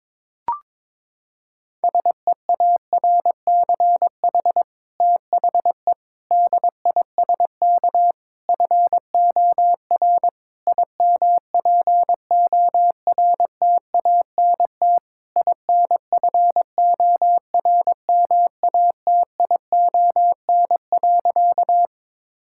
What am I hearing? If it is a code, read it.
Morse code, heard as a keyed tone, about 22 wpm, 700 Hz. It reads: SEARCH THE DISK FOR IMPORTANT INFORMATION.
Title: Sentences from Top 1000 Words - Review Part 6 - 22wpm